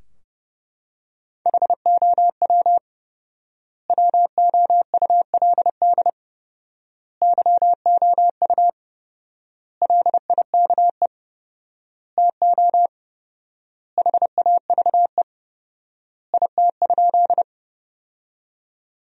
Morse code, heard as a keyed tone, about 32 wpm, 700 Hz. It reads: HOW WOULD YOU LIKE TO HAVE IT?
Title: 4x Word-Spacing - Sentences from Top 100 Words - 30wpm